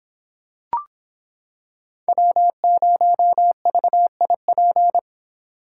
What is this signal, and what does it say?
Morse code, heard as a keyed tone, about 26 wpm, 700 Hz. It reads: W0VIP